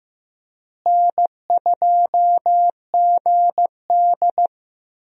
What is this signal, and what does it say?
Morse code, heard as a keyed tone, about 15 wpm, 700 Hz. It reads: N2GD
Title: USA Call Signs - 4 Characters - Repeated 2x - 15wpm